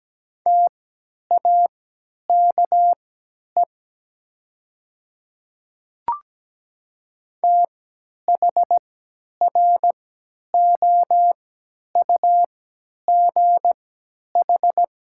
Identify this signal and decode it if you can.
Morse code, heard as a keyed tone, about 17 wpm, spaced as for 10 wpm, 700 Hz. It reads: TAKE THROUGH
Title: 3x Character Spacing - Top 200 Words - 17wpm